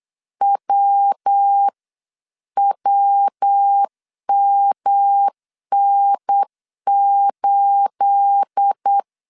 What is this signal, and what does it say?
Morse code, heard as a keyed tone, about 8 wpm, 800 Hz. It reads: WWMN8
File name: CW / DET / 1 9 1500C S B 8wpm 5624